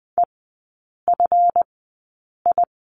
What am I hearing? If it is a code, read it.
Morse code, heard as a keyed tone, about 20 wpm, spaced as for 8 wpm, 700 Hz. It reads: EFI
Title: 138 Lesson 23e - Letter f - Three Characters 20wpm